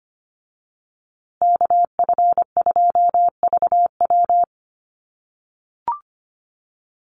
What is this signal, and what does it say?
Morse code, heard as a keyed tone, about 25 wpm, 700 Hz. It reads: KF2VW
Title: USA Call Signs Repeated 3x 25wpm